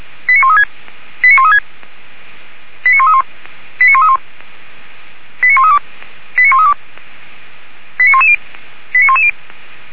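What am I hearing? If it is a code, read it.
Five-tone selective call (ZVEI), heard as quick runs of five tones, tones of about 70 ms. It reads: 89136 89136 89131 89131 89132 89132 89110 89110
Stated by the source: ZVEI five-tone selective call received by SDR, recorded by Feuerwehr Orlamünde.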